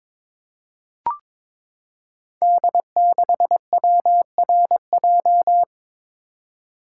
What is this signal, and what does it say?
Morse code, heard as a keyed tone, about 22 wpm, 700 Hz. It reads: D6WRJ
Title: Code Groups - 5 Letter:Numbers - 22wpm